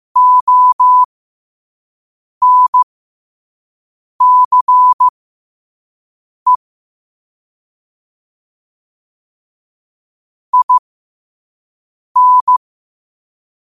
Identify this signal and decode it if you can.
Morse code, heard as a keyed tone, about 15 wpm, spaced as for 5 wpm, 1000 Hz. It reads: ONCE IN
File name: qotdfiles / 5wpm